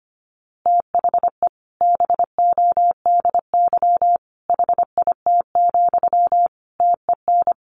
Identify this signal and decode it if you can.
Morse code, heard as a keyed tone, about 25 wpm, 700 Hz. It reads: THE BODY HIT, TEN
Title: A Sound Of Thunder 25wpm